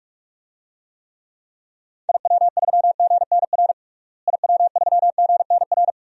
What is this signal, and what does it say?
Morse code, heard as a keyed tone, about 44 wpm, 700 Hz. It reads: IW3GNR IW3GNR
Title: DXCC-100 Call Signs - 6 Characters - Repeated 2x - 45wpm